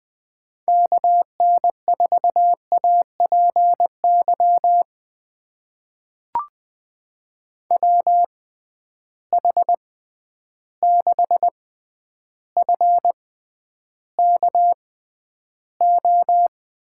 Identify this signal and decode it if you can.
Morse code, heard as a keyed tone, about 20 wpm, spaced as for 10 wpm, 700 Hz. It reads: KN4APY WH6FKO
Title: USA Call Signs - 6 Characters - 6x Character Spacing - 20wpm